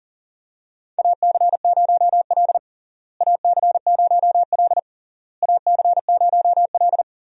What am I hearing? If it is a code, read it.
Morse code, heard as a keyed tone, about 40 wpm, 700 Hz. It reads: AC0L AC0L AC0L
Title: USA Call Signs - 4 Characters - Repeated 3x - 40wpm